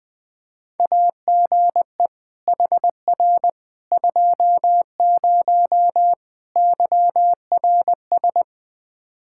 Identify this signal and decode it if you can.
Morse code, heard as a keyed tone, about 20 wpm, 700 Hz. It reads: AGE HR 20 YRS
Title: K1USN SST QSO Practice 20wpm